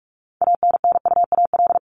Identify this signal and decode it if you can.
Morse code, heard as a keyed tone, about 44 wpm, 700 Hz. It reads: ANNUAL
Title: Top 1251-1500 Words 45wpm